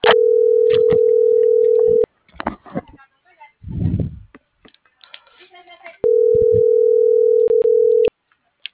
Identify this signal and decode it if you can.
Ringback tone (North America), 440 + 480 Hz.